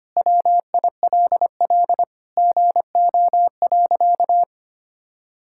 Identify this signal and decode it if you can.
Morse code, heard as a keyed tone, about 25 wpm, 700 Hz. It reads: WILL GO.
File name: practice / 25wpm / Sentences with 4-Word Phrases - 25wpm